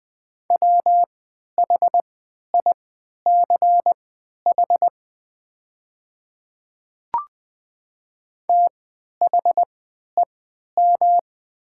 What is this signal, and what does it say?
Morse code, heard as a keyed tone, about 20 wpm, spaced as for 11 wpm, 700 Hz. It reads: WHICH THEM
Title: 3x Character Spacing - Top 300 Words - 20wpm